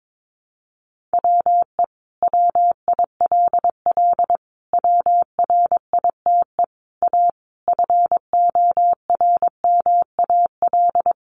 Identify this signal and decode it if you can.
Morse code, heard as a keyed tone, about 22 wpm, 700 Hz. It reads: WE WILL WRITE A FORMAL
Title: Sentences from Top 1300 Words 22wpm